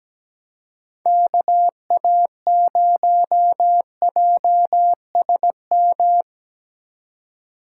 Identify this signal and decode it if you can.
Morse code, heard as a keyed tone, about 20 wpm, 700 Hz. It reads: KA0JSM
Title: Speed Racing - USA 2x3 Call Signs 17wpm